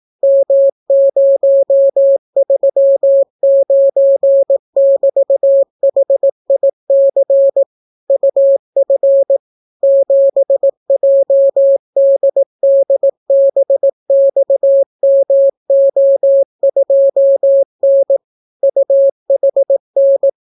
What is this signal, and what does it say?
Morse code, heard as a keyed tone, about 18 wpm, 550 Hz. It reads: M039<BT>HIC UF 7JDDBXMO2N UHN